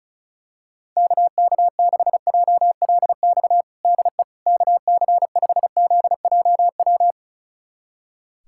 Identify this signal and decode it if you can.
Morse code, heard as a keyed tone, about 36 wpm, 700 Hz. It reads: KK6JLX DE KC5ZJW
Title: USA Callsign DE Callsign 35wpm